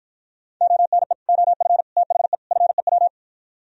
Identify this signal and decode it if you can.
Morse code, heard as a keyed tone, about 52 wpm, 700 Hz. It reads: ONE OF THE FEW